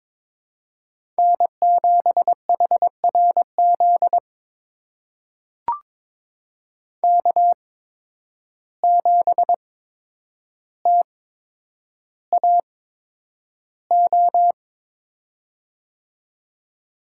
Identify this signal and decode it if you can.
Morse code, heard as a keyed tone, about 22 wpm, spaced as for 9 wpm, 700 Hz. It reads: N7HRZ K7TAO
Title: USA Call Signs - 5 Characters - 8x Character Spacing - 22wpm